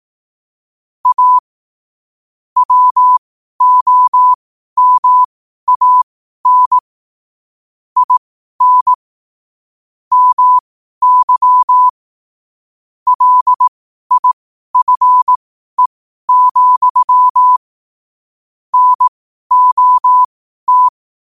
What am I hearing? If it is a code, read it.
Morse code, heard as a keyed tone, about 18 wpm, spaced as for 12 wpm, 1000 Hz. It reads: A WOMAN IN MY LIFE, NOT